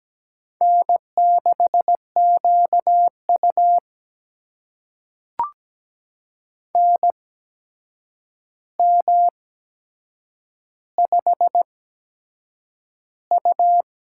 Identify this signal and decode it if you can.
Morse code, heard as a keyed tone, about 17 wpm, spaced as for 7 wpm, 700 Hz. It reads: N6QU NM5U
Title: USA Call Signs - 4 Characters - 8x Character Spacing - 17wpm